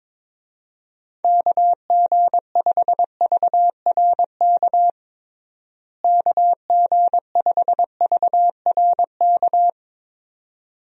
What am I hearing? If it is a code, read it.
Morse code, heard as a keyed tone, about 22 wpm, 700 Hz. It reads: KG5VRK KG5VRK
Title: USA Call Signs - 6 Characters - Repeated 2x - 22wpm